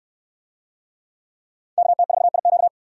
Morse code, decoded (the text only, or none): KT4IQ